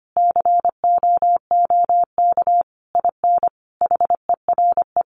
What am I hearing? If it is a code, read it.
Morse code, heard as a keyed tone, about 25 wpm, 700 Hz. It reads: COOK IN HERE
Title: Sentences from Top 1100 Words 25wpm